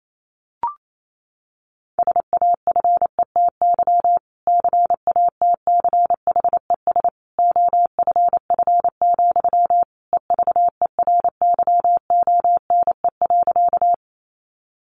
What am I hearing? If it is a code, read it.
Morse code, heard as a keyed tone, about 28 wpm, 700 Hz. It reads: SAFETY CATCHES OFF, EVERYONE.
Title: A Sound Of Thunder 28wpm